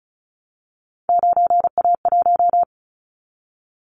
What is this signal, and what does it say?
Morse code, heard as a keyed tone, about 36 wpm, 690 Hz. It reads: 9A1
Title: Callsign Prefixes for Top 50 DXCC - 35wpm